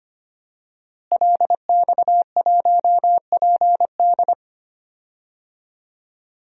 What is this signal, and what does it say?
Morse code, heard as a keyed tone, about 25 wpm, 700 Hz. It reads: LX1PD
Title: DXCC-100 Call Signs - 5 Characters - Repeated 3x - 25wpm